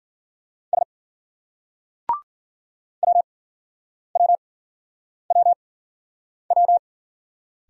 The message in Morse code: S WWWW

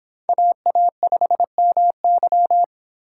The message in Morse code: AA5MY

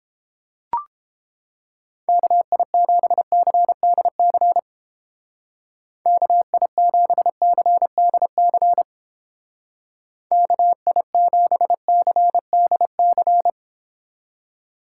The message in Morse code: KI7CDC KI7CDC KI7CDC

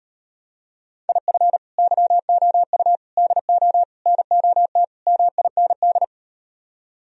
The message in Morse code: IF YOU DO NOT MIND

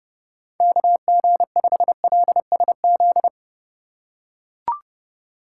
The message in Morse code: KG5LSZ

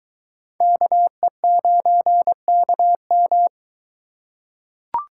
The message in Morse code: KE9KM

